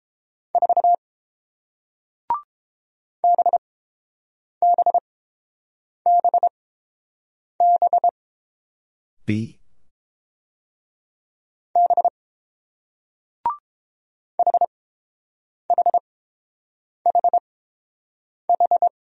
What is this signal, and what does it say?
Morse code, heard as a keyed tone, about 30 wpm, spaced as for 7 wpm, 700 Hz. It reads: 4 BBBB B HHHH